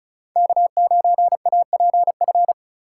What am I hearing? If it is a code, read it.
Morse code, heard as a keyed tone, about 45 wpm, 700 Hz. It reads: K9APF